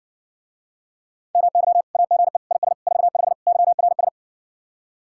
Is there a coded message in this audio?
Yes: Morse code from 1.4 to 4.1 s, heard as a keyed tone, about 60 wpm, 700 Hz. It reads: MY AGE IS 45 YRS